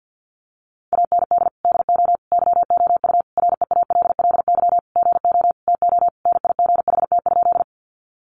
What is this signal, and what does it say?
Morse code, heard as a keyed tone, about 50 wpm, 700 Hz. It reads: AND DO YOU REALLY GO TO NIGHT?
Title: Sentences from Top 600 Words 50wpm